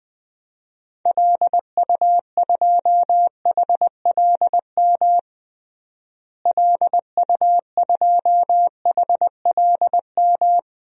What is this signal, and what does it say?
Morse code, heard as a keyed tone, about 20 wpm, 700 Hz. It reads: LU2HLM LU2HLM
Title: DXCC-100 Call Signs - 6 Characters - Repeated 3x - 20wpm